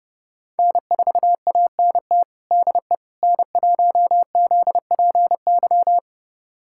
Morse code, heard as a keyed tone, about 30 wpm, 700 Hz. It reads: N4ANT DE N1ZPY
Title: USA Callsign DE Callsign - 5 Characters - 30wpm